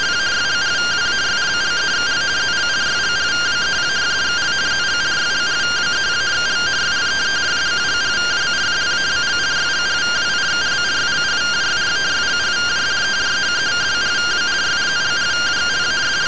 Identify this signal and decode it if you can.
Two-tone teleprinter signal, about 170 Hz shift.